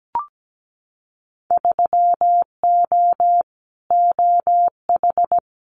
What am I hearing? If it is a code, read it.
Morse code, heard as a keyed tone, about 17 wpm, 700 Hz. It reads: 3O OH
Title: Winter Field Day Exchanges 17wpm